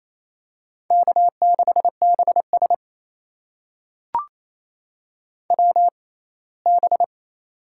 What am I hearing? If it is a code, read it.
Morse code, heard as a keyed tone, about 28 wpm, spaced as for 19 wpm, 700 Hz. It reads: K6BS WB
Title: USA Call Signs - 6x Character Spacing - 28wpm